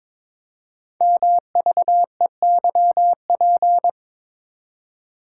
Morse code, heard as a keyed tone, about 22 wpm, 700 Hz. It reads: MVEYP
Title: Code Groups - 5 Letters - 22wpm